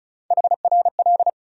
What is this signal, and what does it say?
Morse code, heard as a keyed tone, about 36 wpm, 700 Hz. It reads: HRL
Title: USA Call Sign Postfixes - 3 Letters 35wpm